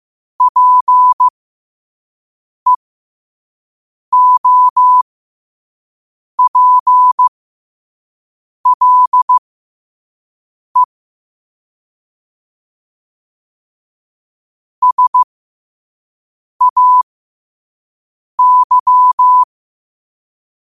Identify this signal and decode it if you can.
Morse code, heard as a keyed tone, about 15 wpm, spaced as for 5 wpm, 1000 Hz. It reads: PEOPLE SAY
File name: qotdfiles / 5wpm